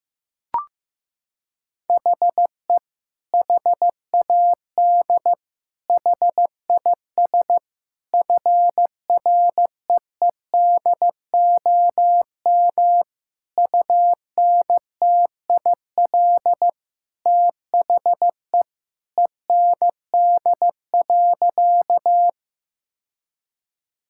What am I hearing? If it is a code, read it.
Morse code, heard as a keyed tone, about 15 wpm, 700 Hz. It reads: HE HAD HIS FREEDOM UNTIL THE END.
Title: Sentences from Top 1000 Words - Review Part 5 - 15wpm